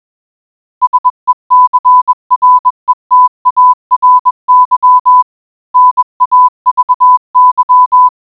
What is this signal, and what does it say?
Morse code, heard as a keyed tone, about 21 wpm, 1000 Hz. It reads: SECRETARY NAVY